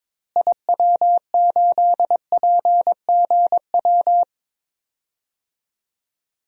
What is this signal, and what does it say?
Morse code, heard as a keyed tone, about 22 wpm, 700 Hz. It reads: IW8PGW
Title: DXCC-100 Call Signs - 6 Characters - Repeated 2x - 22wpm